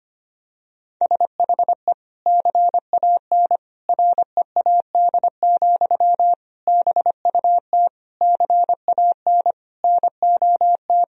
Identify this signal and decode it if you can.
Morse code, heard as a keyed tone, about 25 wpm, 700 Hz. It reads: SHE CAN READ, BUT CAN NOT
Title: Sentences from Top 1000 Words - Review Part 1 - 25wpm